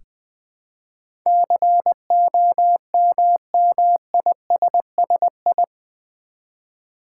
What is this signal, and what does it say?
Morse code, heard as a keyed tone, about 20 wpm, 700 Hz. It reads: COMMISSI
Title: Build A Word - 4 Letters At A Time - 10-Letter Words - 20wpm